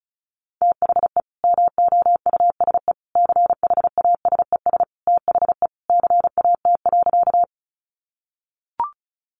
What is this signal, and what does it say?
Morse code, heard as a keyed tone, about 34 wpm, 700 Hz. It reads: THE MOUSE CHASES THE CAT.
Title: Easy Phrases 35wpm